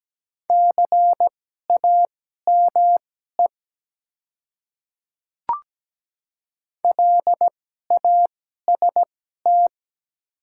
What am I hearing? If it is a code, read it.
Morse code, heard as a keyed tone, about 17 wpm, spaced as for 12 wpm, 690 Hz. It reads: CAME LAST
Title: 2x Character Spacing - Top 300 Words - 17wpm